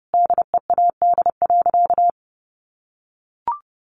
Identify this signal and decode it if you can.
Morse code, heard as a keyed tone, about 30 wpm, 700 Hz. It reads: DEAD.
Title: Sentences from Top 1500 Words 30wpm